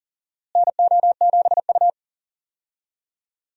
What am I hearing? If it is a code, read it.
Morse code, heard as a keyed tone, about 40 wpm, 700 Hz. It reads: NO7U
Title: USA Call Signs - 4 Characters - Repeated 2x - 40wpm